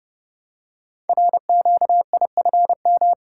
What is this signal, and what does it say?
Morse code, heard as a keyed tone, about 30 wpm, 690 Hz. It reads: RQIFM
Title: Code Groups - 5 Letters - 30wpm